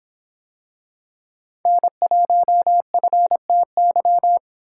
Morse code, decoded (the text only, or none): N1FTY